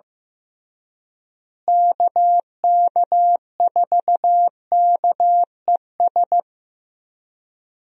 Morse code, heard as a keyed tone, about 15 wpm, 700 Hz. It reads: KK4KES
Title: USA Call Signs - 6 Characters - Repeated 2x - 15wpm